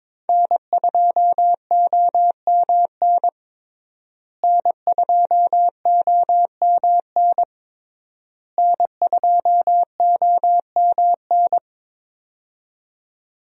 Morse code, heard as a keyed tone, about 22 wpm, 700 Hz. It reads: N2OMN N2OMN N2OMN
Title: USA Call Signs - 5 Characters - Repeated 3x - 22wpm